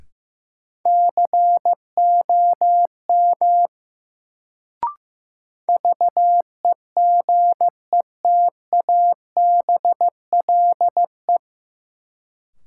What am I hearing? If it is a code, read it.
Morse code, heard as a keyed tone, about 15 wpm, 700 Hz. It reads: COM VEGETABLE
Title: Sentences from Top 1300 Words - New Word Drill Alt - 15wpm